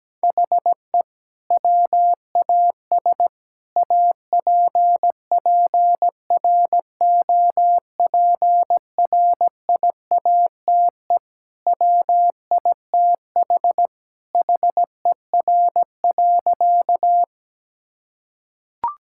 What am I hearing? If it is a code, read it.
Morse code, heard as a keyed tone, about 17 wpm, 700 Hz. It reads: HE WAS APPROPRIATE WITH HER.
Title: Sentences from Top 1000 Words - Review Part 5 - 17wpm